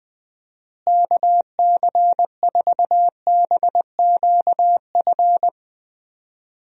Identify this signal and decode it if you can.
Morse code, heard as a keyed tone, about 20 wpm, 700 Hz. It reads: KC4BQF
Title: USA Call Signs - 6 Characters - Repeated 3x - 20wpm